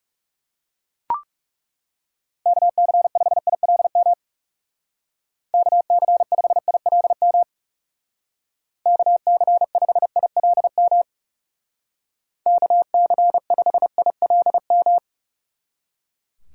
Morse code, heard as a keyed tone, about 36 wpm, 700 Hz. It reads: KC5ILM KC5ILM KC5ILM KC5ILM
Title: Speed Racing - USA 2x3 Call Signs 30wpm